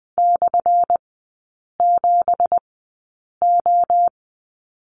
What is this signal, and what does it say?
Morse code, heard as a keyed tone, about 20 wpm, spaced as for 8 wpm, 700 Hz. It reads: /7O
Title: 178 Lesson 29e - Slash - Three Characters 20wpm